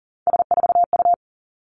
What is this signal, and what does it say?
Morse code, heard as a keyed tone, about 41 wpm, 700 Hz. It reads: S4U
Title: Three Challenging Letter-Number Combinations 40wpm